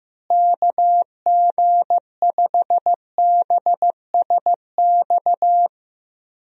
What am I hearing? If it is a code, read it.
Morse code, heard as a keyed tone, about 16 wpm, 700 Hz. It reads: KG5BSX
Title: USA Call Signs - 6 Characters - Repeated 3x - 15wpm